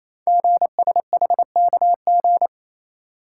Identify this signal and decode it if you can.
Morse code, heard as a keyed tone, about 29 wpm, 700 Hz. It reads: GSHKG
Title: Code Groups - 5 Letters - 28wpm